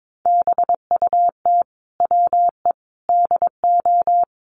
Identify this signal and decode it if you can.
Morse code, heard as a keyed tone, about 22 wpm, 700 Hz. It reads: BUT WE DO
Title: Sets of 5 Words Encore2 22wpm